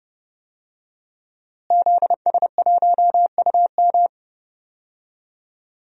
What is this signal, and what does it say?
Morse code, heard as a keyed tone, about 30 wpm, 700 Hz. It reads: ZS1UM
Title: DXCC-100 Call Signs - 5 Characters - Repeated 3x - 30wpm